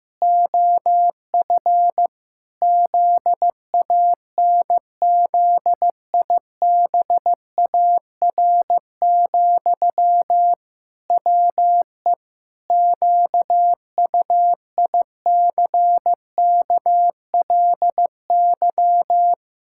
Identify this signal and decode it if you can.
Morse code, heard as a keyed tone, about 15 wpm, 700 Hz. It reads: OF ZANZIBAR, WE QUICKLY
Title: Pangram Challenge v2 - 15wpm